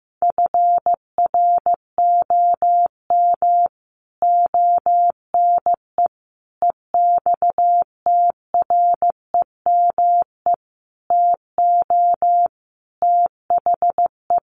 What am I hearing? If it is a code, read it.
Morse code, heard as a keyed tone, about 15 wpm, 700 Hz. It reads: FROM ONE EXTREME TO THE
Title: Sentences from Top 1300 Words 15wpm